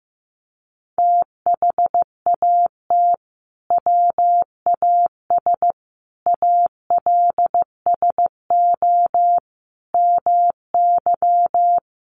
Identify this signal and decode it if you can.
Morse code, heard as a keyed tone, about 15 wpm, 700 Hz. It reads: THAT WAS ALSO MY
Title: Sentences from Top 1200 Words 15wpm